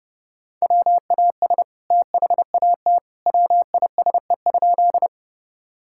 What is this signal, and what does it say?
Morse code, heard as a keyed tone, about 28 wpm, 700 Hz. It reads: WAS THAT WISE?
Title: Sentences from Top 1600 Words 30wpm